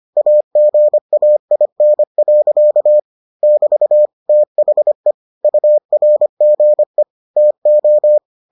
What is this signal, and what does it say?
Morse code, heard as a keyed tone, about 25 wpm, 600 Hz. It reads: AGAIN. <BT> THE URGE TO